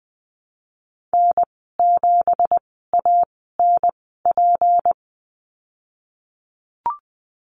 Morse code, heard as a keyed tone, about 20 wpm, spaced as for 15 wpm, 700 Hz. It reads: N7ANP
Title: 186 Lesson 30g - Letter b - Call Signs 20wpm